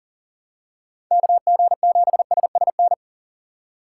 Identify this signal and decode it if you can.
Morse code, heard as a keyed tone, about 40 wpm, 700 Hz. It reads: KG7SSN
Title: USA Call Signs - 6 Characters - 6x Character Spacing - 40wpm